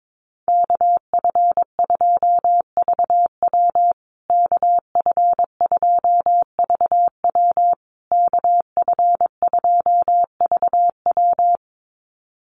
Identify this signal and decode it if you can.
Morse code, heard as a keyed tone, about 22 wpm, 700 Hz. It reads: KF2VW KF2VW KF2VW
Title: USA Call Signs Repeated 3x 22wpm